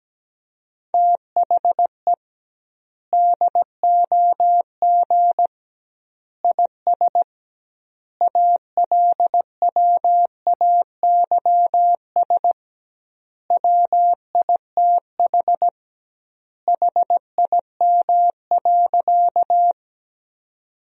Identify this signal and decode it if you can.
Morse code, heard as a keyed tone, about 17 wpm, 700 Hz. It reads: THE DOG IS ALWAYS WITH HIM.
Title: 2x Word-Spacing - Sentences from Top 400 Words - 17wpm